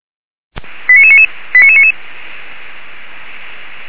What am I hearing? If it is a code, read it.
Five-tone selective call (ZVEI), heard as quick runs of five tones, tones of about 80 ms. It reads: 89999 89999